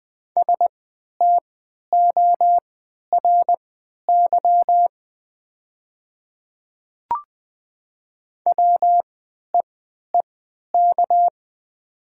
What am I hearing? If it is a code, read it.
Morse code, heard as a keyed tone, about 20 wpm, spaced as for 11 wpm, 700 Hz. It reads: STORY WEEK
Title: 3x Character Spacing - Top 500 Words - 20wpm